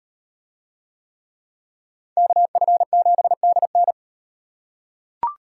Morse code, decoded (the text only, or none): KF7DN